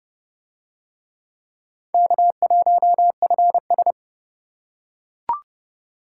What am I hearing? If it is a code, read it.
Morse code, heard as a keyed tone, about 26 wpm, 700 Hz. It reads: K1FS